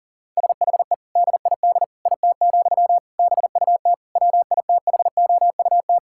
Morse code, heard as a keyed tone, about 40 wpm, 700 Hz. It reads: SHE DID IT, BUT WITHOUT
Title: Sentences from Top 1000 Words - Review Part 6 - 40wpm